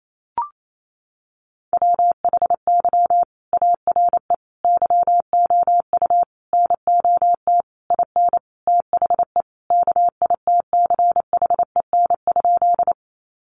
Morse code, heard as a keyed tone, about 28 wpm, 700 Hz. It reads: WHY ARE YOU NOT IN THE KITCHEN?
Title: Sentences from Top 1400 Words 28wpm